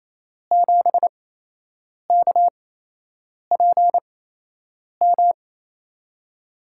Morse code, heard as a keyed tone, about 28 wpm, spaced as for 8 wpm, 700 Hz. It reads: 7KPM
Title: USA Call Signs - 6 Characters - 8x Character Spacing - 28wpm